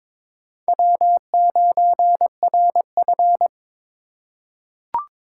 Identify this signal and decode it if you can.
Morse code, heard as a keyed tone, about 22 wpm, 700 Hz. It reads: W9RF
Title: USA Call Signs - 4 Characters - 8x Character Spacing - 22wpm